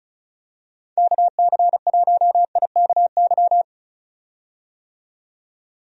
Morse code, heard as a keyed tone, about 36 wpm, 700 Hz. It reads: KC1IKY
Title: USA Call Signs - 6 Characters - 35wpm